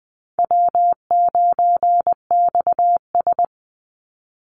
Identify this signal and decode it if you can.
Morse code, heard as a keyed tone, about 20 wpm, 700 Hz. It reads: W9XS
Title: Easy Call Signs - 20wpm